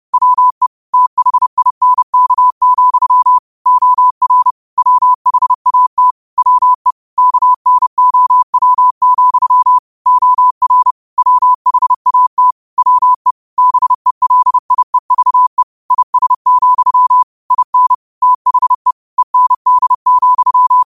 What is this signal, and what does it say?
Morse code, heard as a keyed tone, about 30 wpm, 1000 Hz. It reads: WE THINK, OR WHAT WE KNOW, OR WHAT WE BELIEVE IS, IN THE END,
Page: New quotes every day in morse code at 30 Words per minute.